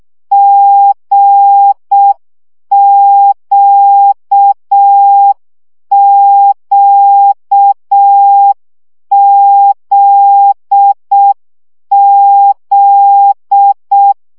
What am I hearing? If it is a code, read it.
Morse code, heard as a keyed tone, about 6 wpm, 800 Hz. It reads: GQQZZ
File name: CW gyakorlas g z q 5 baud